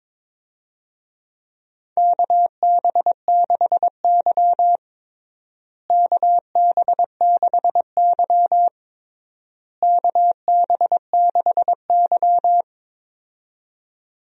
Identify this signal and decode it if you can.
Morse code, heard as a keyed tone, about 22 wpm, 700 Hz. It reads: KB6Y KB6Y KB6Y
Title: USA Call Signs - 4 Characters - Repeated 3x - 22wpm